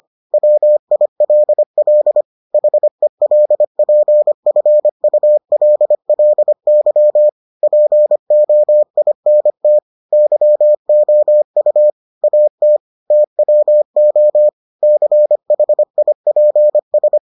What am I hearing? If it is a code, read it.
Morse code, heard as a keyed tone, about 25 wpm, 600 Hz. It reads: WILL HELPFULLY POINT YOU AT TWO CHIPS